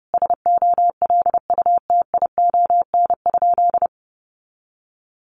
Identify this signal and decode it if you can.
Morse code, heard as a keyed tone, about 30 wpm, 700 Hz. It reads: SOLUTION?
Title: Sentences from Top 1300 Words 30wpm